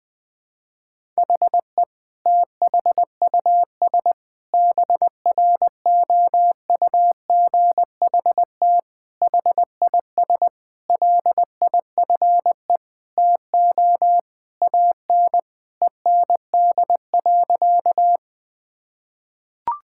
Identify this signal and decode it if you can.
Morse code, heard as a keyed tone, about 20 wpm, 700 Hz. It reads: HE THUS BROUGHT HIS LIFE TO AN END.
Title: Sentences from Top 1000 Words - Review Part 5 - 20wpm